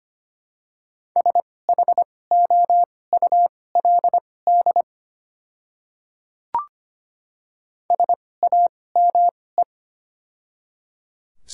Morse code, heard as a keyed tone, about 25 wpm, spaced as for 18 wpm, 700 Hz. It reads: SHOULD SAME
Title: 2x Character Spacing - Top 300 Words - 25wpm